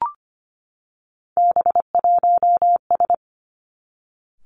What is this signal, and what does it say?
Morse code, heard as a keyed tone, about 25 wpm, 700 Hz. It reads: B1S
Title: Three Challenging Letter-Number Combinations 25wpm